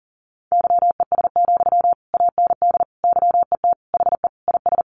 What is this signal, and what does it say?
Morse code, heard as a keyed tone, about 40 wpm, 700 Hz. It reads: YES, AND YET HE IS
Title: Sentences from Top 700 Words 40wpm